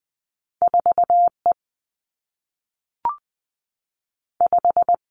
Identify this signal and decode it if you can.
Morse code, heard as a keyed tone, about 20 wpm, 700 Hz. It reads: 4 E 5
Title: Two Challenging Letter-Number Combinations 20wpm